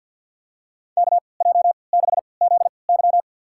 Morse code, heard as a keyed tone, about 53 wpm, spaced as for 29 wpm, 700 Hz. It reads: KJ6ZY
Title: USA Call Signs - 5 Characters - 3x Character Spacing - 50wpm